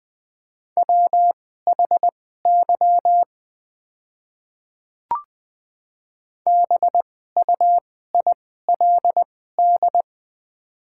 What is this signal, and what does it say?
Morse code, heard as a keyed tone, about 20 wpm, spaced as for 14 wpm, 700 Hz. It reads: WHY BUILD
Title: 2x Character Spacing - Top 200 Words - 20wpm